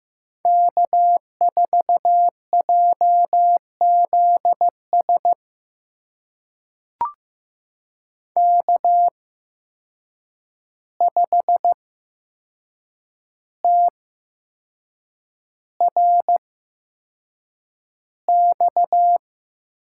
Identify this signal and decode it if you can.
Morse code, heard as a keyed tone, about 15 wpm, spaced as for 6 wpm, 700 Hz. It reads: K4JZS K5TRX
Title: USA Call Signs - 5 Characters - 8x Character Spacing - 15wpm